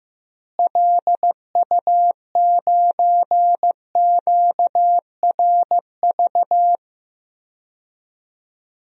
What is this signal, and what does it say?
Morse code, heard as a keyed tone, about 15 wpm, 700 Hz. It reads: LU9QRV
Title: DXCC-100 Call Signs - 6 Characters - Repeated 2x - 15wpm